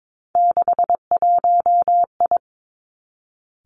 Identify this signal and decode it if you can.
Morse code, heard as a keyed tone, about 22 wpm, 700 Hz. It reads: 61I
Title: Three Challenging Letter-Number Combinations 22wpm